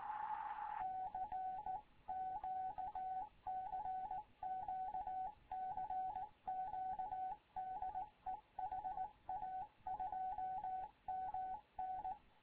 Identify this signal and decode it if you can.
Morse code, heard as a keyed tone, about 18 wpm, 750 Hz, with radio noise. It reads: CQ CQ CQ DE HA2MN
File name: Morze-HA2MN / hangp / 2 11 GYAKORLAT